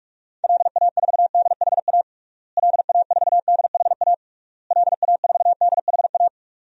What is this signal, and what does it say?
Morse code, heard as a keyed tone, about 46 wpm, 700 Hz. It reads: LA4DHA LA4DHA LA4DHA
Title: DXCC-100 Call Signs - 6 Characters - Repeated 3x - 45wpm